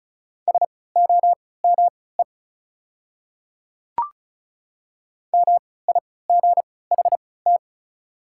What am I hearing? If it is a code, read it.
Morse code, heard as a keyed tone, about 36 wpm, spaced as for 20 wpm, 690 Hz. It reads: SOME MIGHT
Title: 3x Character Spacing - Top 300 Words - 35wpm